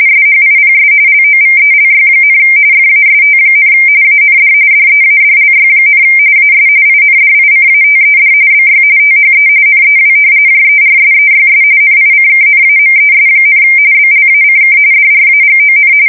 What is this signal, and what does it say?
RTTY, about 45.45 baud. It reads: SHIPS LOVELY, FAST, AND DEADLY, AS INDICATED BY THE NEW JERSEYS NICKNAMES, BLACK DRAGON
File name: fsk170 / battleshipnj